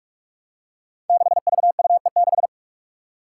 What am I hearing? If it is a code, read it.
Morse code, heard as a keyed tone, about 44 wpm, 700 Hz. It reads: 6VUE6
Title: Five Challenging Letter-Number Combinations 45wpm